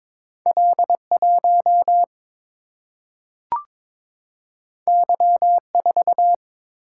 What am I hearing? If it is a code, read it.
Morse code, heard as a keyed tone, about 22 wpm, 700 Hz. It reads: L1 Y4
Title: Single-Letter Single-Number 22wpm